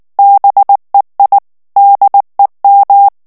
Morse code, heard as a keyed tone, about 19 wpm, 800 Hz. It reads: BEI DEM
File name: morse mp3 / ALLEMAND texte clair 2 vitesse 18 mots minute